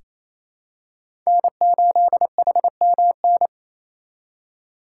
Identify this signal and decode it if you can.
Morse code, heard as a keyed tone, about 28 wpm, 700 Hz. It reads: N8HMN
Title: USA Call Signs - 5 Characters - 4x Character Spacing - 28wpm